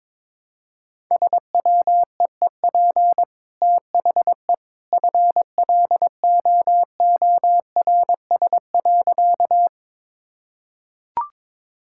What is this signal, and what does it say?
Morse code, heard as a keyed tone, about 22 wpm, 700 Hz. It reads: SWEEP THE FLOORS.
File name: practice / 22wpm / Easy Sentences with Hints - 22wpm